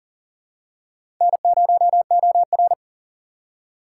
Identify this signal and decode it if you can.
Morse code, heard as a keyed tone, about 38 wpm, 700 Hz. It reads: N0OR
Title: USA Call Signs - 4 Characters - 6x Character Spacing - 40wpm